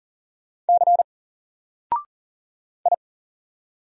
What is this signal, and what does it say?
Morse code, heard as a keyed tone, about 40 wpm, 700 Hz. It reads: C I
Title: Single Letter-Number 40wpm